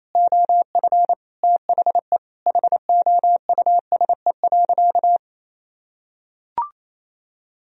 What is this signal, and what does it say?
Morse code, heard as a keyed tone, about 28 wpm, 700 Hz. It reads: OF THE HOUSE.
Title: Sentences from Top 1000 Words - Review Part 2 - 28wpm